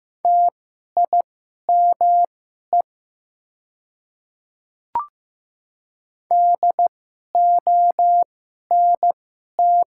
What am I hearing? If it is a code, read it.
Morse code, heard as a keyed tone, about 15 wpm, spaced as for 11 wpm, 700 Hz. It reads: TIME DONT
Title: 2x Character Spacing - Top 300 Words - 15wpm